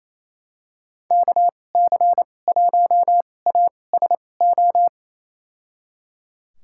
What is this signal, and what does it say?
Morse code, heard as a keyed tone, about 28 wpm, spaced as for 20 wpm, 700 Hz. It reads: KC1ASO
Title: USA Call Signs - 2x Character Spacing - 28wpm